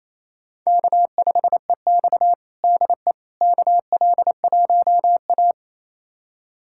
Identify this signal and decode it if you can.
Morse code, heard as a keyed tone, about 28 wpm, 700 Hz. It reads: K5EX DE KL1A
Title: USA Callsign DE Callsign - 4 Characters - 28wpm